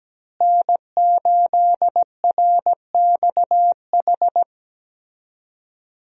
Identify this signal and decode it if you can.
Morse code, heard as a keyed tone, about 17 wpm, 700 Hz. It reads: N8RXH
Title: USA Call Signs - 5 Characters - Repeated 3x - 17wpm